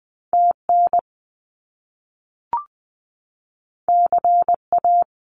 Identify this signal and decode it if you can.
Morse code, heard as a keyed tone, about 20 wpm, 700 Hz. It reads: TN CA
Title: US States 20wpm